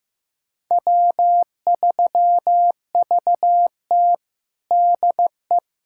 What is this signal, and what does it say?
Morse code, heard as a keyed tone, about 15 wpm, 700 Hz. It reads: W3VT DE
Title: USA Callsign DE Callsign - 4 Characters - 15wpm